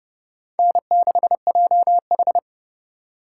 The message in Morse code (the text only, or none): N6JH